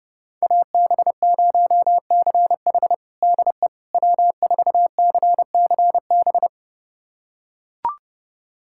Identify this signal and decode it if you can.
Morse code, heard as a keyed tone, about 32 wpm, 700 Hz. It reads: AB0CH DE W4CCB
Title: USA Callsign DE Callsign - 5 Characters - 30wpm